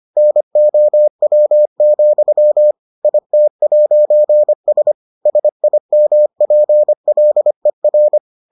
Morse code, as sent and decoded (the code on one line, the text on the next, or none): -. --- .-- --..-- / .. - .----. ... / ... .. -- .--. .-.. .-.
NOW, IT'S SIMPLER